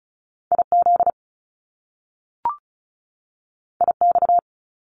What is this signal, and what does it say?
Morse code, heard as a keyed tone, about 36 wpm, 700 Hz. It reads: IZ IX
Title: Top Two Letter Combinations in Words 35wpm